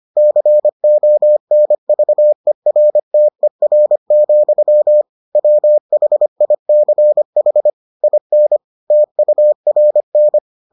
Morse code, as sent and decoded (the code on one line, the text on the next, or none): -.-. --- -. ...- . .-. - . .-. --..-- / .-- .... .. -.-. .... / .. -. / - ..- .-. -.
CONVERTER, WHICH IN TURN